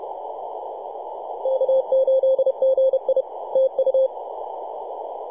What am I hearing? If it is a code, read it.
Morse code, heard as a keyed tone, about 31 wpm, 530 Hz, with radio noise. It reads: K8GI TU
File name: posts / 2015-oct-03-2319z-tx3x-21-023mhz-cw